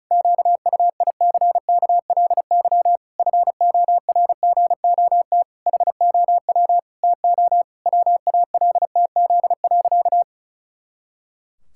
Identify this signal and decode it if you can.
Morse code, heard as a keyed tone, about 34 wpm, 700 Hz. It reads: QUICKLY FORGOT HOW TO WALTZ.
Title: Pangram Challenge v2 - 35wpm